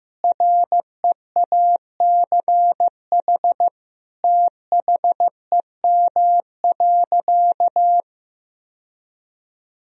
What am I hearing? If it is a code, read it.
Morse code, heard as a keyed tone, about 16 wpm, 700 Hz. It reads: REACH THEM.
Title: Sentences from Top 1000 Words - Review Part 2 - 15wpm